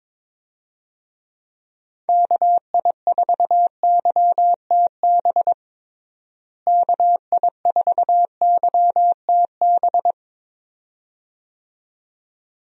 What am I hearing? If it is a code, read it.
Morse code, heard as a keyed tone, about 22 wpm, 700 Hz. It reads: KI4YTB KI4YTB
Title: USA Call Signs - 6 Characters - Repeated 2x - 22wpm